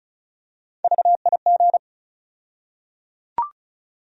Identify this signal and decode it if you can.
Morse code, heard as a keyed tone, about 34 wpm, 700 Hz. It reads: VIG
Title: USA Call Sign Postfixes - 3 Letters 35wpm